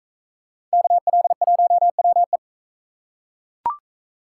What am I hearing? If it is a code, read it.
Morse code, heard as a keyed tone, about 40 wpm, 700 Hz. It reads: KL1WE